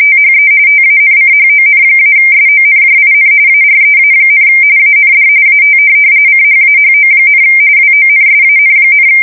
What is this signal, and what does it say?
RTTY, about 45.45 baud. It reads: THE TANCHON-SONJIN AREA, AND AT CHAHO, WONSAN,
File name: fsk170 / USS-Missouri